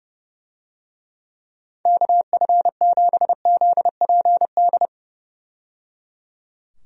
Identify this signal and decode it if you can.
Morse code, heard as a keyed tone, about 30 wpm, 700 Hz. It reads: KF7ZPD
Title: USA Call Signs - 6 Characters - 30wpm